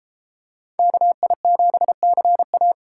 Morse code, heard as a keyed tone, about 31 wpm, 700 Hz. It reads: KI7CA